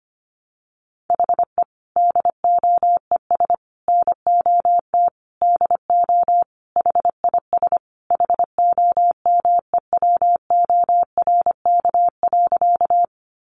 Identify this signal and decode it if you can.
Morse code, heard as a keyed tone, about 25 wpm, 700 Hz. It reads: HE DOES NOT DO HIS HOMEWORK.
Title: Additional Easy Phrases 25wpm